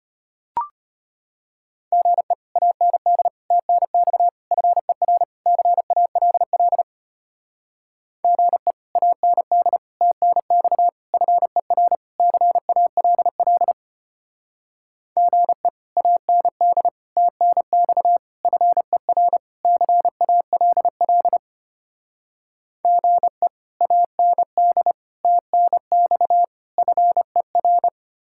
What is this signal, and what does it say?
Morse code, heard as a keyed tone, about 32 wpm, 700 Hz. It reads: GE AND TNX FER CALL GE AND TNX FER CALL GE AND TNX FER CALL GE AND TNX FER